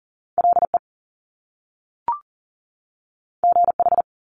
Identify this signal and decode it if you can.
Morse code, heard as a keyed tone, about 40 wpm, 700 Hz. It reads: LE GH
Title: Top Two Letter Combinations in Words 40wpm